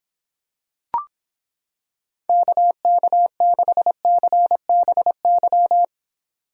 Morse code, heard as a keyed tone, about 26 wpm, 700 Hz. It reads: KK6CBY